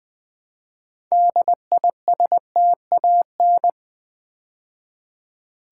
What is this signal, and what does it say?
Morse code, heard as a keyed tone, about 20 wpm, 700 Hz. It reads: DISTAN
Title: Build A Word - 3 Letters At A Time - 8-Letter Words - 20wpm